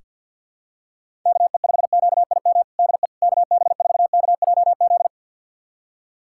Morse code, heard as a keyed tone, about 46 wpm, 700 Hz. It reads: KE5QIM DE KB4KJZ
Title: USA Callsign DE Callsign - 6 Characters - 50wpm